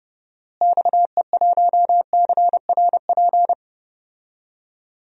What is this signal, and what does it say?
Morse code, heard as a keyed tone, about 31 wpm, 700 Hz. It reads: XE1CRP
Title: DXCC-100 Call Signs - 6 Characters - Repeated 3x - 30wpm